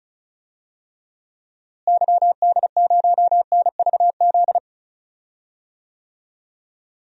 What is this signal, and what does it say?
Morse code, heard as a keyed tone, about 36 wpm, 700 Hz. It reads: YD0NVZ
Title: DXCC-100 Call Signs - 6 Characters - Repeated 3x - 35wpm